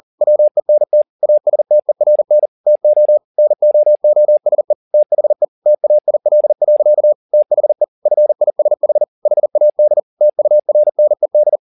Morse code, heard as a keyed tone, about 40 wpm, 600 Hz. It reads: WENT ASTERN TO NOOSE THE TAIL. THE FISH HAD TURNED
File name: the old man and the sea 2 40wpm